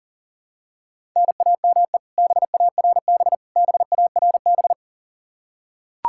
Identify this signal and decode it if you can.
Morse code, heard as a keyed tone, about 40 wpm, 700 Hz. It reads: NAME BARB BARB
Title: Practice QSO - Exchange 1 - Beginning - 40wpm